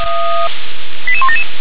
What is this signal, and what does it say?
Five-tone selective call (ZVEI), heard as quick runs of five tones, tones of about 70 ms. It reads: 88177